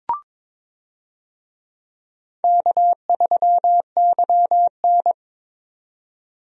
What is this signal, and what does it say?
Morse code, heard as a keyed tone, about 22 wpm, 700 Hz. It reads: K3YN
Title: USA Call Signs - 4 Characters - 22wpm